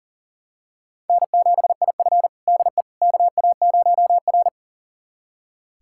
Morse code, heard as a keyed tone, about 40 wpm, 700 Hz. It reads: N7IF DE KA0R
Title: USA Callsign DE Callsign - 4 Characters - 40wpm